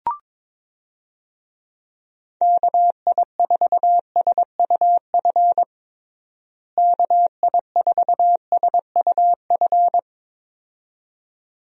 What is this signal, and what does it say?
Morse code, heard as a keyed tone, about 22 wpm, 700 Hz. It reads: KI4SUF KI4SUF
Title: USA Call Signs - 6 Characters - Repeated 2x - 22wpm